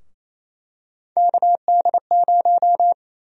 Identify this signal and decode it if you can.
Morse code, heard as a keyed tone, about 28 wpm, 700 Hz. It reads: KD0
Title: USA Call Sign Prefixes - 28wpm